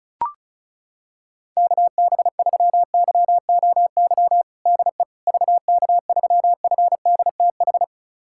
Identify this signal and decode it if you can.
Morse code, heard as a keyed tone, about 36 wpm, 700 Hz. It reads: KB3YOY DE VK3FDTH
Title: USA Callsign DE Callsign 35wpm